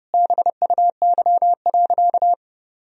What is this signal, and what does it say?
Morse code, heard as a keyed tone, about 30 wpm, 700 Hz. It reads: BUY.
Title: Sentences from Top 1000 Words - Review Part 4 - 30wpm